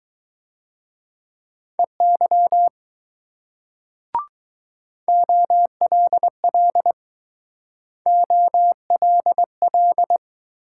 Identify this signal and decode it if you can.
Morse code, heard as a keyed tone, about 19 wpm, 700 Hz. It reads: EY OLL OLL